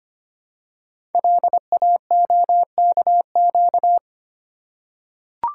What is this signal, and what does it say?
Morse code, heard as a keyed tone, about 25 wpm, 700 Hz. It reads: LAOKQ
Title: Code Groups - 5 Letters - 25wpm